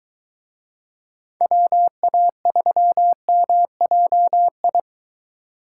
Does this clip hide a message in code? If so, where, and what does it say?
Yes: Morse code from 1.4 to 4.8 s, heard as a keyed tone, about 23 wpm, 700 Hz. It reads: WA3MJI